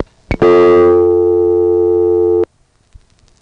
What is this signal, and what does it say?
Dial tone (North America), 350 + 440 Hz.
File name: phonetrips / HarrisonbgVA1HQ